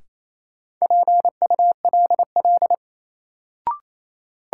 Morse code, heard as a keyed tone, about 28 wpm, 700 Hz. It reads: PULL
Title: Sentences from Top 400 Words - New Word Drill - 28wpm